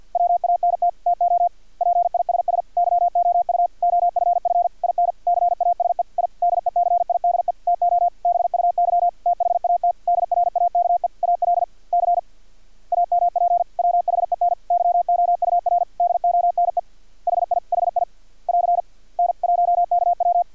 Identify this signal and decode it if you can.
Morse code, heard as a keyed tone, about 50 wpm, 700 Hz. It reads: WANT TO PISS YOU OFF IN CASE I DECIDE TO BUY THAT DRAKE ALC AMP WHEN YOUR DONE HI HI <AR> N1KW
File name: archived / CRSnet